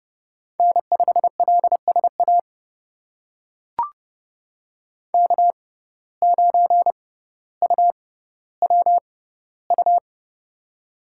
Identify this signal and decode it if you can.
Morse code, heard as a keyed tone, about 30 wpm, spaced as for 15 wpm, 700 Hz. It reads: N5LSA K9UWU
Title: USA Call Signs - 5 Characters - 6x Character Spacing - 30wpm